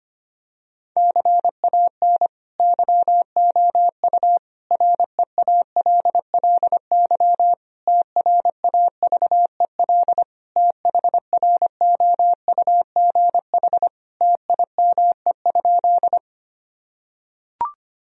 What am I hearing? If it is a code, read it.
Morse code, heard as a keyed tone, about 25 wpm, 700 Hz. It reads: CAN YOU REALLY TRAVEL THROUGH TIME?
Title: Sentences from Top 1000 Words - Review Part 2 - 25wpm